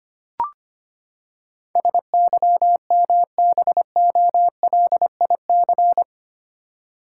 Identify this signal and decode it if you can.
Morse code, heard as a keyed tone, about 25 wpm, 700 Hz. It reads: SYMBOLIC
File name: Eight Letter Words - 25wpm